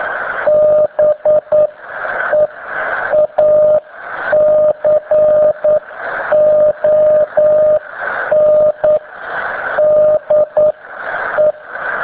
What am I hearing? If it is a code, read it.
Morse code, heard as a keyed tone, about 9 wpm, spaced as for 8 wpm, 610 Hz, with radio noise. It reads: BEACON DE